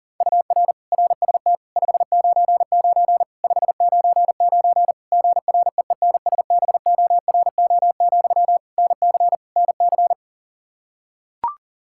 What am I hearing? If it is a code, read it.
Morse code, heard as a keyed tone, about 40 wpm, 700 Hz. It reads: UR RST 599 599 GREENSBORO, NC NC
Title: Practice QSO - Exchange 1 - Beginning - 40wpm